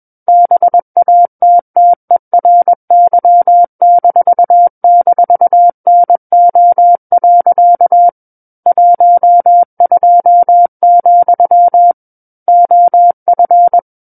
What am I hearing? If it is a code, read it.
Morse code, heard as a keyed tone, about 21 wpm, 700 Hz. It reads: BATTERY--NO. 12, OF
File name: Book One / 12-Chapter 12 - 21 WPM